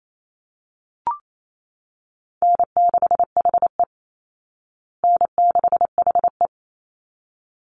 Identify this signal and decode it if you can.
Morse code, heard as a keyed tone, about 28 wpm, 700 Hz. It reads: N6HE N6HE
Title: Easy Call Signs Repeated 2x - 28wpm